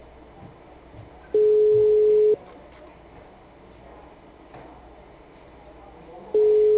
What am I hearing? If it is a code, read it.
Ringback tone (Europe, 425 Hz), with speech.